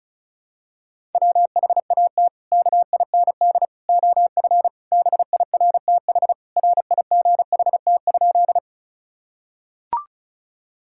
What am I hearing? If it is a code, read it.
Morse code, heard as a keyed tone, about 34 wpm, 700 Hz. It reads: WHAT KIND OF BIRTH RIGHT?
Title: Sentences from Top 1000 Words - Review Part 5 - 35wpm